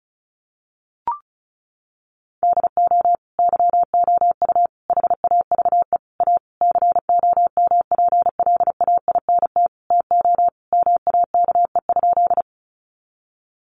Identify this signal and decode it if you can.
Morse code, heard as a keyed tone, about 34 wpm, 700 Hz. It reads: DO YOU HAVE A COMPLAINT TO MAKE?
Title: Sentences from Top 1400 Words 35wpm